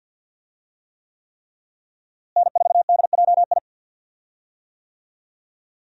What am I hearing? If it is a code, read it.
Morse code, heard as a keyed tone, about 47 wpm, 700 Hz. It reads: N4DJI
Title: USA Call Signs - 5 Characters - Repeated 3x - 50wpm